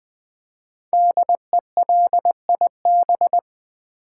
Morse code, heard as a keyed tone, about 20 wpm, 700 Hz. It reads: DELIB
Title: Build A Word - 5 Letters At A Time - 12-Letter Words - 20wpm